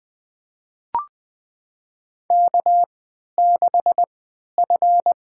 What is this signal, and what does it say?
Morse code, heard as a keyed tone, about 20 wpm, spaced as for 11 wpm, 700 Hz. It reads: K6F
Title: USA Call Signs - 3x Character Spacing - 20wpm